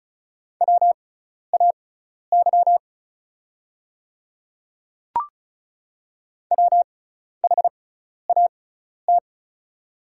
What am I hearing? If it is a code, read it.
Morse code, heard as a keyed tone, about 34 wpm, spaced as for 12 wpm, 700 Hz. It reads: WAY WHAT
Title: 6x Character Spacing - Top 100 Words - 35wpm